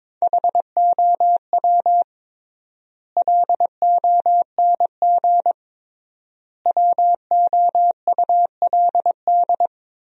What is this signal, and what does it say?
Morse code, heard as a keyed tone, about 22 wpm, 700 Hz. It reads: HOW LONG WOULD
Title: 3x Word-Spacing - Sentences from Top 500 Words - 22wpm